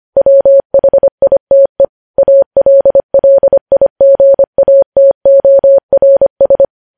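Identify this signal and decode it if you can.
Morse code, heard as a keyed tone, about 25 wpm, 550 Hz. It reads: WHITE ALLIGATORS
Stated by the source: Morse code news headlines sent at 25.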